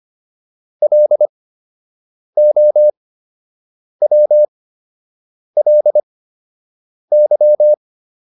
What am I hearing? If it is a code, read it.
Morse code, heard as a keyed tone, about 25 wpm, spaced as for 7 wpm, 600 Hz. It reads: LOWLY